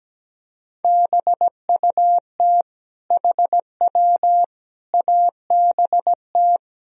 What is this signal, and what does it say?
Morse code, heard as a keyed tone, about 17 wpm, 700 Hz. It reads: BUT HW ABT
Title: Sentences from Top 100 Words and Abbreviations 17wpm